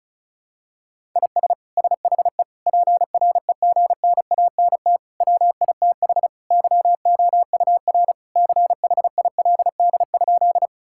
Morse code, heard as a keyed tone, about 36 wpm, 700 Hz. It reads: IS SHE PREGNANT WITH YOUR CHILD?
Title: Sentences from Top 1000 Words - Review Part 6 - 35wpm